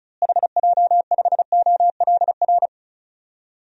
Morse code, heard as a keyed tone, about 36 wpm, 700 Hz. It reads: HJ5OLR
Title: DXCC-100 Call Signs - 6 Characters - Repeated 3x - 35wpm